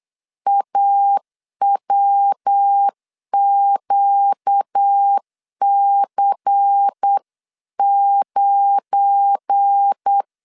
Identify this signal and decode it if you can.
Morse code, heard as a keyed tone, about 8 wpm, 800 Hz. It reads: AWQC9
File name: CW / DET / 1 9 1500C S A 8wpm 5637